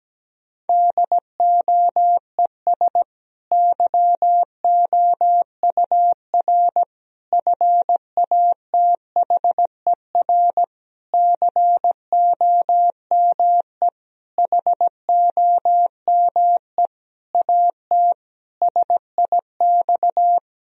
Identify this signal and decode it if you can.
Morse code, heard as a keyed tone, about 17 wpm, 700 Hz. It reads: DOES YOUR FATHER COME HOME AT SIX
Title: Sentences from Top 1000 Words - Review Part 2 - 17wpm